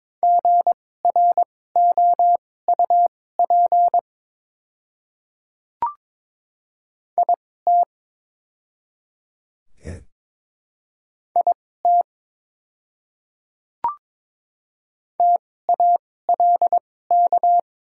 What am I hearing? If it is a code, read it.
Morse code, heard as a keyed tone, about 22 wpm, spaced as for 16 wpm, 700 Hz. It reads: GROUP IT IT TALK
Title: 2x Character Spacing - Top 500 Words - 22wpm